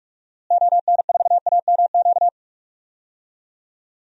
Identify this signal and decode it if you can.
Morse code, heard as a keyed tone, about 46 wpm, 700 Hz. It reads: ON4AMQ
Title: DXCC-100 Call Signs - 6 Characters - Repeated 2x - 45wpm